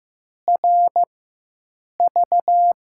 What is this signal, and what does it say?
Morse code, heard as a keyed tone, about 15 wpm, spaced as for 7 wpm, 700 Hz. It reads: RV
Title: USA Call Signs - 6 Characters - 4x Character Spacing - 15wpm